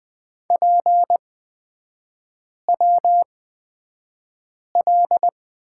Morse code, heard as a keyed tone, about 20 wpm, spaced as for 5 wpm, 700 Hz. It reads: PWL
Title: 243 Lesson 39e - Letter z - Three Characters 20wpm